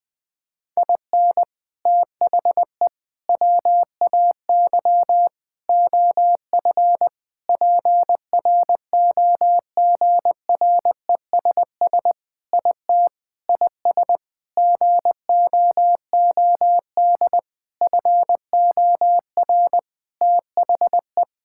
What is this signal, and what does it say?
Morse code, heard as a keyed tone, about 20 wpm, 700 Hz. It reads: IN THE WAY OF PROGRESS IT IS GOOD FOR THE
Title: Sentences with 4-Word Phrases - Spoken Beginning - 20wpm